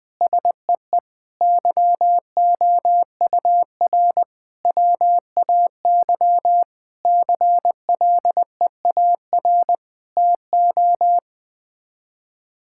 Morse code, heard as a keyed tone, about 20 wpm, 700 Hz. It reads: SEE YOUR WAY CLEAR TO
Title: Sets of 9 Words - 20wpm